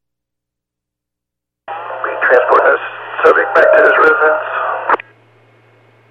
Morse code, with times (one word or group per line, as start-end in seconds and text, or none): none